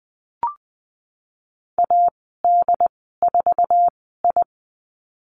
Morse code, heard as a keyed tone, about 20 wpm, spaced as for 14 wpm, 700 Hz. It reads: AD4I
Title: 140 Lesson 23g - Letter f - Call Signs 20wpm